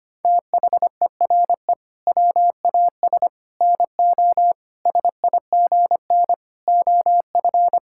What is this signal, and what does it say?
Morse code, heard as a keyed tone, about 25 wpm, 700 Hz. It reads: THERE WAS NO SIGN OF
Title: Sentences from Top 1000 Words - Review Part 6 - 25wpm